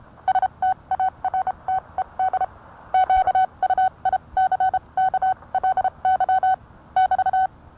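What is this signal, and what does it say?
Morse code, heard as a keyed tone, about 29 wpm, spaced as for 24 wpm, 750 Hz. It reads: STARTED QUICKLY <BT>
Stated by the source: Morse code sent by N1EA on a Vibroplex key.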